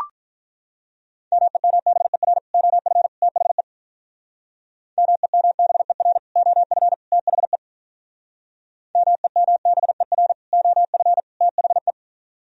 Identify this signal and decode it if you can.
Morse code, heard as a keyed tone, about 44 wpm, 700 Hz. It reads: MEMBER OF THE MEMBER OF THE MEMBER OF THE